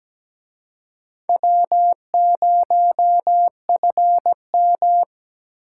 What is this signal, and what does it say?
Morse code, heard as a keyed tone, about 20 wpm, 700 Hz. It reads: W0FM
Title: Speed Racing - USA 1x2 Call Signs 17wpm